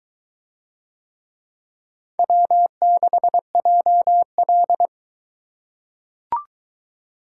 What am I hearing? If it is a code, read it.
Morse code, heard as a keyed tone, about 23 wpm, 700 Hz. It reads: W6JL